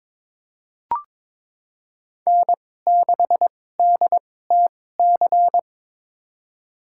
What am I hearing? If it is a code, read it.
Morse code, heard as a keyed tone, about 22 wpm, spaced as for 16 wpm, 700 Hz. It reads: N6DTC
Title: USA Call Signs - 5 Characters - 2x Character Spacing - 22wpm